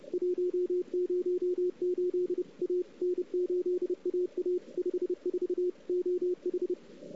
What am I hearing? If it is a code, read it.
Morse code, heard as a keyed tone, about 30 wpm, 360 Hz, with radio noise. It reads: 108 A N8AA 54 OH
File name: CW / 2008-ss-cw-k5tr-op-n5rz -00:00